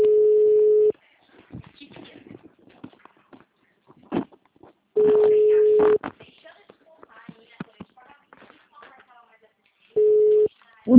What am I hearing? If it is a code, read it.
Ringback tone (Europe, 425 Hz), with speech.
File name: out- -2002- - - .8